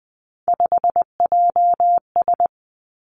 Morse code, heard as a keyed tone, about 20 wpm, 700 Hz. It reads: 5JS
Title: Three Challenging Letter-Number Combinations 20wpm